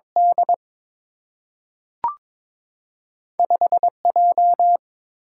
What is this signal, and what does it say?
Morse code, heard as a keyed tone, about 22 wpm, 700 Hz. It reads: D 5 J
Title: Single-Letter Single-Number 22wpm